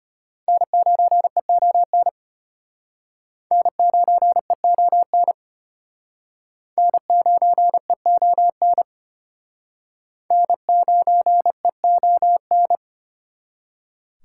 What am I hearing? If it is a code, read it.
Morse code, heard as a keyed tone, about 31 wpm, 700 Hz. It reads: N9EON N9EON N9EON N9EON
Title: Speed Racing - USA 1x3 Call Signs 25wpm